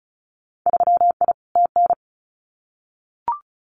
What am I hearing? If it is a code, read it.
Morse code, heard as a keyed tone, about 34 wpm, 690 Hz. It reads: 3I TN
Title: Winter Field Day Exchanges 35wpm